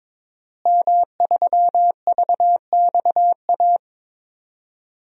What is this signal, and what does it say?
Morse code, heard as a keyed tone, about 21 wpm, 700 Hz. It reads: M3VXA
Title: DXCC-100 Call Signs - 5 Characters - Repeated 2x - 22wpm